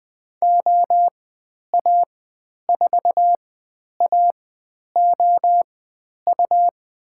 Morse code, heard as a keyed tone, about 20 wpm, spaced as for 10 wpm, 700 Hz. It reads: OA4AOU
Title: 109 Lesson 18g - Letter m - Call Signs 20wpm